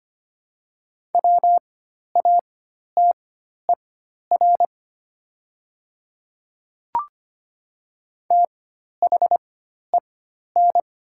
Morse code, heard as a keyed tone, about 25 wpm, spaced as for 12 wpm, 700 Hz. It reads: WATER THEN
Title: 4x Character Spacing - Top 100 Words - 25wpm